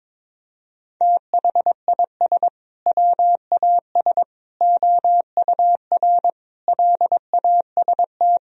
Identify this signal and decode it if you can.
Morse code, heard as a keyed tone, about 22 wpm, 700 Hz. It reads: THIS WAS OUR LAST
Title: Sentences from Top 1000 Words - Review Part 1 - 22wpm